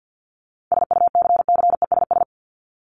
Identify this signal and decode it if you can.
Morse code, heard as a keyed tone, about 50 wpm, 700 Hz. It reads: SUCCESS
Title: Top 751-1000 Words 50wpm